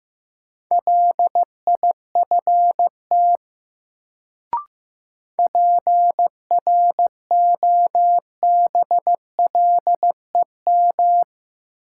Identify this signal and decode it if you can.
Morse code, heard as a keyed tone, about 15 wpm, 700 Hz. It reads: LIFT PROBLEM
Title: Sentences from Top 1000 Words - New Word Drill - 15wpm